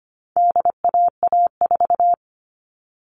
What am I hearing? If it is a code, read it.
Morse code, heard as a keyed tone, about 25 wpm, 700 Hz. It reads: DAA4
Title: All Cut Numbers - 4-Digit 25wpm